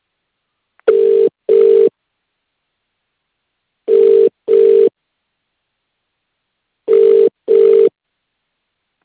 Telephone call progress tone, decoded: ringback tone